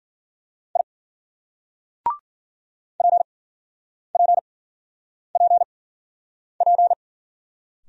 Morse code, heard as a keyed tone, about 50 wpm, spaced as for 9 wpm, 700 Hz. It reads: I PPPP